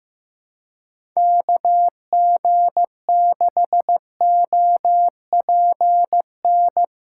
Morse code, heard as a keyed tone, about 15 wpm, 700 Hz. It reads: KG6OPN
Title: USA Callsign DE Callsign 15wpm